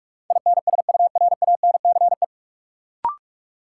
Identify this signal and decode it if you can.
Morse code, heard as a keyed tone, about 46 wpm, 700 Hz. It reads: INSURANCE
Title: Sentences from Top 1100 Words - New Word Drill - 45wpm